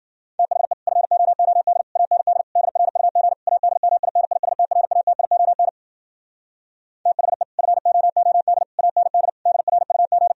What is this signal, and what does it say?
Morse code, heard as a keyed tone, about 51 wpm, 700 Hz. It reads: THE FOOD AND DRUG ADMINISTRATION THE FOOD AND DRUG